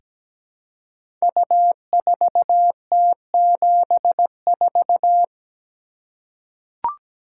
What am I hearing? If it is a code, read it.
Morse code, heard as a keyed tone, about 17 wpm, 700 Hz. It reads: U4T74
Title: Code Groups - 5 Letter:Numbers - 17wpm